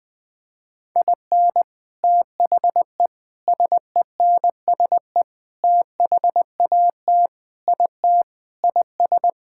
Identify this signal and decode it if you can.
Morse code, heard as a keyed tone, about 19 wpm, 700 Hz. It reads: IN THE SENSE THAT IT IS
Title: Sentences with 4-Word Phrases - 20wpm